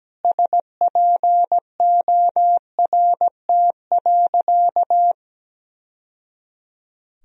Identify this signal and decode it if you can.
Morse code, heard as a keyed tone, about 17 wpm, 700 Hz. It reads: SPORT.
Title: Sentences from Top 1000 Words - Review Part 4 - 17wpm